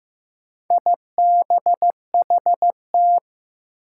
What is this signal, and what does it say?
Morse code, heard as a keyed tone, about 15 wpm, 700 Hz. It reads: IBHT
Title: Four Challenging Letter-Number Combinations 15wpm